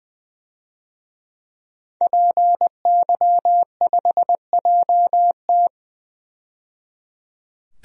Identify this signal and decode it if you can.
Morse code, heard as a keyed tone, about 20 wpm, 700 Hz. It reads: PY5JT
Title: DXCC-100 Call Signs - 5 Characters - 20wpm